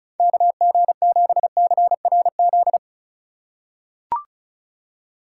Morse code, heard as a keyed tone, about 34 wpm, 700 Hz. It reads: KG7CRZ
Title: USA Call Signs - 6 Characters - 4x Character Spacing - 35wpm